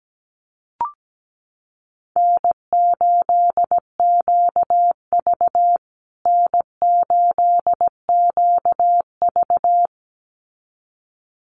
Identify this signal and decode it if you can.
Morse code, heard as a keyed tone, about 17 wpm, 700 Hz. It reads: N8QV N8QV
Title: USA Call Signs Repeated 2x 17wpm